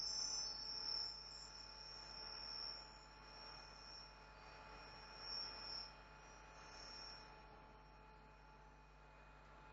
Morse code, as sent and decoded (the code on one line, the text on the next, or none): none